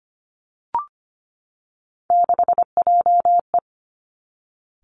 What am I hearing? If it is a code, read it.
Morse code, heard as a keyed tone, about 25 wpm, 700 Hz. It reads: 6JE
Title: Three Challenging Letter-Number Combinations 25wpm